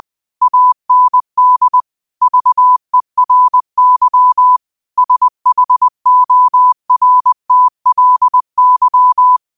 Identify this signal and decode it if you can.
Morse code, heard as a keyed tone, about 20 wpm, 1000 Hz. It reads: AND VERY SHORTLY